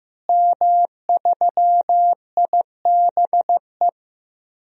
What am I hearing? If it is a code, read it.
Morse code, heard as a keyed tone, about 16 wpm, 700 Hz. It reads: M3IBE
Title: DXCC-100 Call Signs - 5 Characters - Repeated 3x - 15wpm